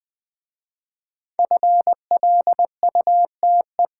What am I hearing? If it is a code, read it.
Morse code, heard as a keyed tone, about 20 wpm, 700 Hz. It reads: FLUTE
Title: Easy Words with Hints - 20wpm